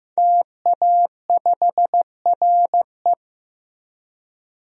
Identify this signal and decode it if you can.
Morse code, heard as a keyed tone, about 15 wpm, 700 Hz. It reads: TA5RE
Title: DXCC-100 Call Signs - 5 Characters - 15wpm